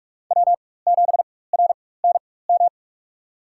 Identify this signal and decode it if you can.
Morse code, heard as a keyed tone, about 45 wpm, spaced as for 21 wpm, 700 Hz. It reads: W7RNM
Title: USA Call Signs - 5 Characters - 4x Character Spacing - 45wpm